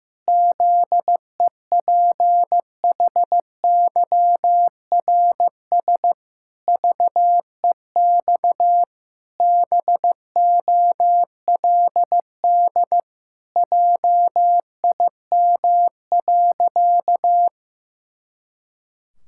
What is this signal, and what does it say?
Morse code, heard as a keyed tone, about 16 wpm, 700 Hz. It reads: ZEPHYRS VEX BOLD JIM.
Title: Pangram Challenge v2 - 15wpm